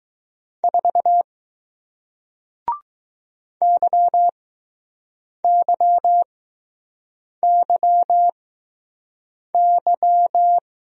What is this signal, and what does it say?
Morse code, heard as a keyed tone, about 20 wpm, spaced as for 6 wpm, 700 Hz. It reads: 4 YYYY